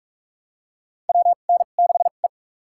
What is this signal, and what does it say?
Morse code, heard as a keyed tone, about 46 wpm, spaced as for 33 wpm, 700 Hz. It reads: WN6E
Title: USA Call Signs - 4 Characters - 2x Character Spacing - 45wpm